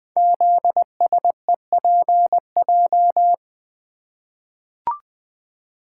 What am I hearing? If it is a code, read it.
Morse code, heard as a keyed tone, about 20 wpm, 700 Hz. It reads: ZSEPJ
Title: Code Groups - 5 Letters - 20wpm